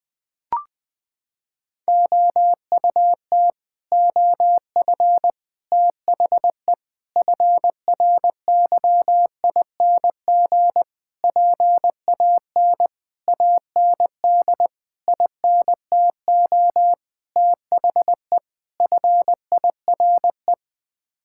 Morse code, heard as a keyed tone, about 20 wpm, 700 Hz. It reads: OUT OF THE FRYING PAN AND INTO THE FIRE
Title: Sets of 9 Words - 20wpm